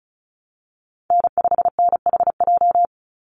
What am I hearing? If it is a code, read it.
Morse code, heard as a keyed tone, about 34 wpm, 700 Hz. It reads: N5NHJ
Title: Call Signs Repeated 2x - 35wpm